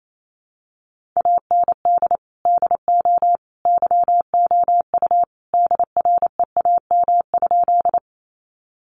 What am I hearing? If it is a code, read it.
Morse code, heard as a keyed tone, about 28 wpm, 700 Hz. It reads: AND DO YOU DREAM?
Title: Sentences from Top 1500 Words 28wpm